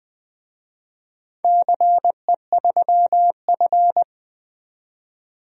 Morse code, heard as a keyed tone, about 20 wpm, 700 Hz. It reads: CE3F
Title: DXCC-100 Call Signs - 4 Characters - 20wpm